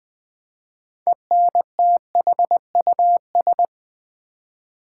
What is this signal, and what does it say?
Morse code, heard as a keyed tone, about 20 wpm, 700 Hz. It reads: ENTHUS
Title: Build A Word - 6 Letters At A Time - 12-Letter Words - 20wpm